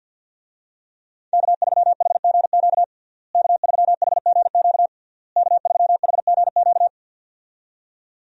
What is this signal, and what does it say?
Morse code, heard as a keyed tone, about 46 wpm, 700 Hz. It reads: K3HGQ K3HGQ K3HGQ
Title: USA Call Signs - 5 Characters - Repeated 3x - 50wpm